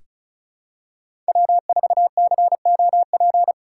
Morse code, heard as a keyed tone, about 34 wpm, 700 Hz. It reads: W4COP
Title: USA Call Signs - 5 Characters - 4x Character Spacing - 35wpm